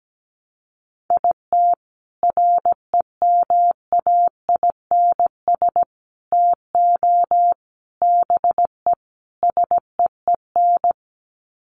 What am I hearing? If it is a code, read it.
Morse code, heard as a keyed tone, about 17 wpm, 700 Hz. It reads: IT REMAINS TO BE SEEN
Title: Sets of 5 Words Encore 17wpm